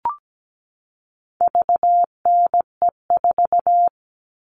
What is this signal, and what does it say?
Morse code, heard as a keyed tone, about 17 wpm, 700 Hz. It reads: VNE4
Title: All Cut Numbers - 4-Digit 17wpm